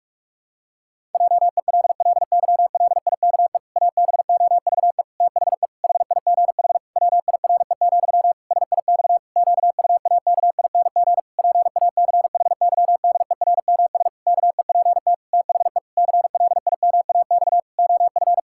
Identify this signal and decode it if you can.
Morse code, heard as a keyed tone, about 43 wpm, 700 Hz. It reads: JELLYLIKE ABOVE THE HIGH WIRE, SIX QUAKING PACHYDERMS KEPT THE CLIMAX OF
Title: Pangram Challenge v2 - 45wpm